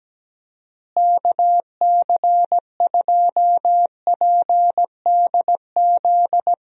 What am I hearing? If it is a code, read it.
Morse code, heard as a keyed tone, about 17 wpm, 700 Hz. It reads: KC2PDZ
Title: USA Call Signs - 6 Characters - Repeated 3x - 17wpm